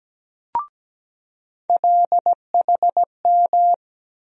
Morse code, heard as a keyed tone, about 17 wpm, 700 Hz. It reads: LHM
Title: USA Call Sign Postfixes - 3 Letters 17wpm